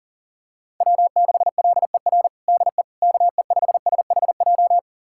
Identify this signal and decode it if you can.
Morse code, heard as a keyed tone, about 39 wpm, 700 Hz. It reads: W6LER DE KE5SHJ
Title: USA Callsign DE Callsign 40wpm